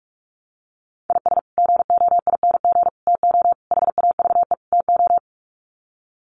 Morse code, heard as a keyed tone, about 44 wpm, 700 Hz. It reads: IS GOING TO HAVE TO
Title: Sets of 5 Words 45wpm